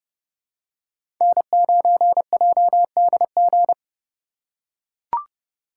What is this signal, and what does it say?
Morse code, heard as a keyed tone, about 30 wpm, 700 Hz. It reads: N9JDG
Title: USA Call Signs - 5 Characters - 3x Character Spacing - 30wpm